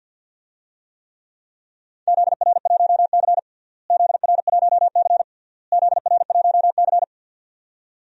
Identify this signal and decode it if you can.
Morse code, heard as a keyed tone, about 46 wpm, 700 Hz. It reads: ZR1C ZR1C ZR1C
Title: DXCC-100 Call Signs - 4 Characters - Repeated 3x - 50wpm